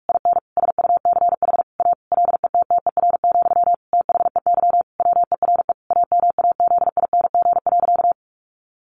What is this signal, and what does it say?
Morse code, heard as a keyed tone, about 44 wpm, 700 Hz. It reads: IN SUCH A LETTER, THEY WERE AMAZING.
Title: Sentences from Top 1500 Words 45wpm